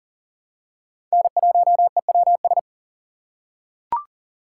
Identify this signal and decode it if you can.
Morse code, heard as a keyed tone, about 40 wpm, 700 Hz. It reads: N1EWS
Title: USA Call Signs - 5 Characters - 4x Character Spacing - 40wpm